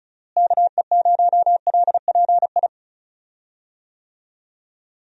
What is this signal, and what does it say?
Morse code, heard as a keyed tone, about 36 wpm, 700 Hz. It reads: KE0LPI
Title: USA Call Signs - 6 Characters - Repeated 3x - 35wpm